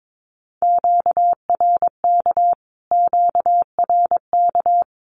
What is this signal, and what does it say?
Morse code, heard as a keyed tone, about 22 wpm, 700 Hz. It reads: QRK QRK
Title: Q-Codes Repeated 3x 22wpm